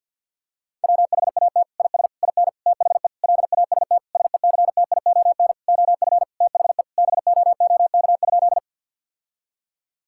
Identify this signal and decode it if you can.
Morse code, heard as a keyed tone, about 51 wpm, 700 Hz. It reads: WHAT IS IN THE LAST SECTION OF THE BOOK?
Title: Sentences from Top 1000 Words - Review Part 5 - 50wpm